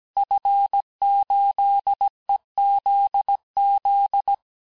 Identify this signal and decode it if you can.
Morse code, heard as a keyed tone, about 17 wpm, 800 Hz. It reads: F8EZZ
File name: cours-du-lundi files / cours-1587-FIN-de-QSO-vitesse-16MPM